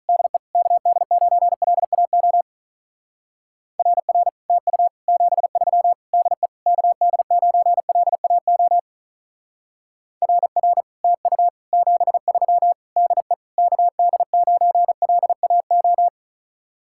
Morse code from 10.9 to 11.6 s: TU

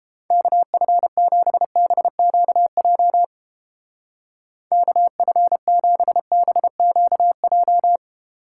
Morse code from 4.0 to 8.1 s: KF7BQJ

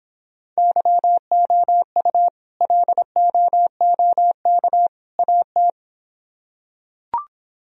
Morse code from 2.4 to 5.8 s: LOOK AT

